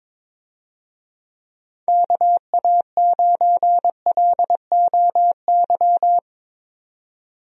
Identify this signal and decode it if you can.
Morse code, heard as a keyed tone, about 22 wpm, 700 Hz. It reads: KA9LOY
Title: USA Call Signs - 6 Characters - 22wpm